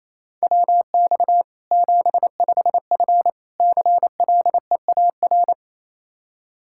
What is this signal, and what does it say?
Morse code, heard as a keyed tone, about 28 wpm, 700 Hz. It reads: WX 75F CLEAR
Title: Practice QSO - Exchange 2 - Info - 28wpm